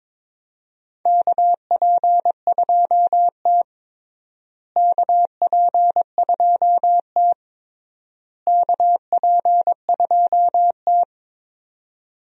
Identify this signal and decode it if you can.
Morse code, heard as a keyed tone, about 22 wpm, 700 Hz. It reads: KP2T KP2T KP2T
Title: USA Call Signs - 4 Characters - Repeated 3x - 22wpm